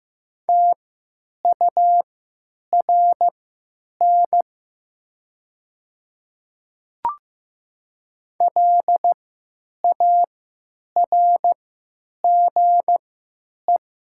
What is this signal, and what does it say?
Morse code, heard as a keyed tone, about 15 wpm, spaced as for 9 wpm, 700 Hz. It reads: TURN LARGE
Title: 3x Character Spacing - Top 200 Words - 15wpm